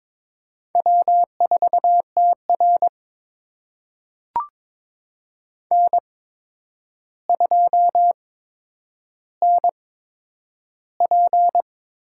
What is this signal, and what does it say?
Morse code, heard as a keyed tone, about 22 wpm, spaced as for 9 wpm, 700 Hz. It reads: W4TR N2NP
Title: USA Call Signs - 4 Characters - 8x Character Spacing - 22wpm